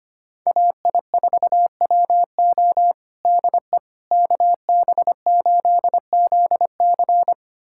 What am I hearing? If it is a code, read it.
Morse code, heard as a keyed tone, about 25 wpm, 700 Hz. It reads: AI4WO DE KB8ZC
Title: USA Callsign DE Callsign - 5 Characters - 25wpm